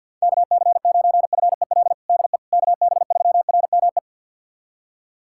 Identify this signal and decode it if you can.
Morse code, heard as a keyed tone, about 51 wpm, 700 Hz. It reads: KC9FEL DE KB3RME
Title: USA Callsign DE Callsign - 6 Characters - 50wpm